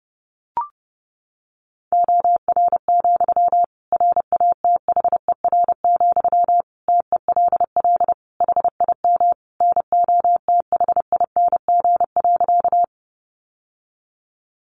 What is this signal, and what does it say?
Morse code, heard as a keyed tone, about 30 wpm, 700 Hz. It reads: OR, RATHER, TELL HIM NOTHING.
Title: Sentences from Top 700 Words 30wpm